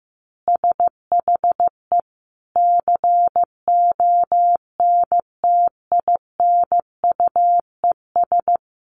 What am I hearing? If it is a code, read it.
Morse code, heard as a keyed tone, about 15 wpm, 700 Hz. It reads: SHE CONTINUES
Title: Easy Phrases 15wpm